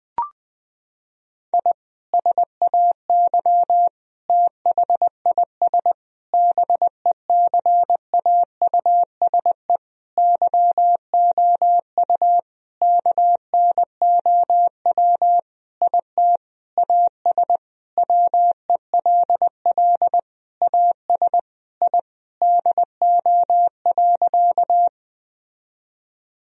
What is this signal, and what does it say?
Morse code, heard as a keyed tone, about 21 wpm, 700 Hz. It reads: I SAY THIS BECAUSE YOU KNOW IT AS WELL AS I DO.
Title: Sentences from Top 1000 Words - Review Part 3 - 20wpm